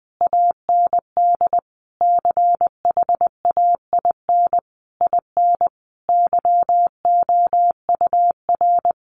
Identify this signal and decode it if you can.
Morse code, heard as a keyed tone, about 20 wpm, 700 Hz. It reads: AND CHAIN IN YOUR
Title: Sentences from Top 1500 Words 20wpm